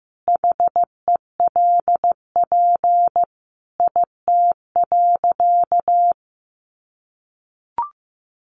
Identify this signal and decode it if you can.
Morse code, heard as a keyed tone, about 15 wpm, 700 Hz. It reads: HELP IT.
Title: Sentences from Top 1100 Words 15wpm